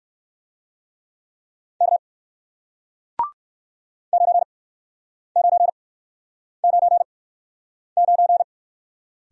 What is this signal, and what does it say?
Morse code, heard as a keyed tone, about 60 wpm, spaced as for 9 wpm, 700 Hz. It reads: K 9999